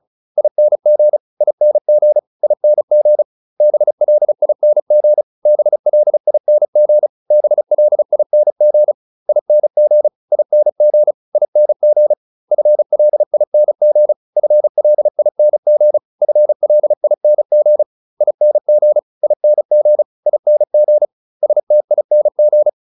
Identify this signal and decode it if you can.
Morse code, heard as a keyed tone, about 34 wpm, 600 Hz. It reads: ING ING ING BLING BLING BLING ING ING ING FLING FLING FLING ING ING ING STING